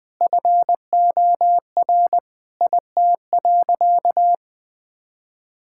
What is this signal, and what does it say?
Morse code, heard as a keyed tone, about 21 wpm, 700 Hz. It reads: FOR IT.
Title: Sentences from Top 1600 Words 20wpm